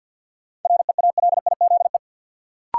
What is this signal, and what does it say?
Morse code, heard as a keyed tone, about 51 wpm, 700 Hz. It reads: REALIZE
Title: Sentences from Top 1000 Words - New Word Drill - 50wpm